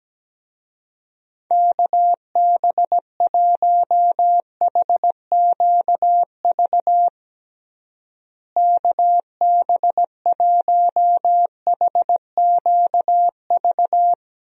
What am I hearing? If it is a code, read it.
Morse code, heard as a keyed tone, about 17 wpm, 700 Hz. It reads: KB1HQV KB1HQV
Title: USA Call Signs - 6 Characters - Repeated 2x - 17wpm